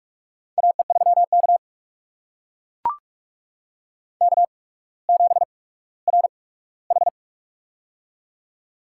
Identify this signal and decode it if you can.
Morse code, heard as a keyed tone, about 43 wpm, spaced as for 19 wpm, 700 Hz. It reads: AE3K K7RH
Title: USA Call Signs - 4 Characters - 8x Character Spacing - 45wpm